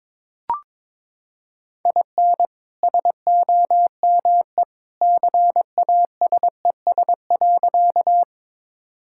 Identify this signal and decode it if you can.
Morse code, heard as a keyed tone, about 22 wpm, 700 Hz. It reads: IN SOME CASES.
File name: practice / 22wpm / Sentences with 2-Word Phrases - 22wpm